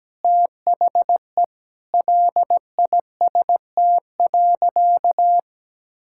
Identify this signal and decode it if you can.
Morse code, heard as a keyed tone, about 17 wpm, 700 Hz. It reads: THE LIST.
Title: Sentences from Top 1000 Words - Review Part 6 - 17wpm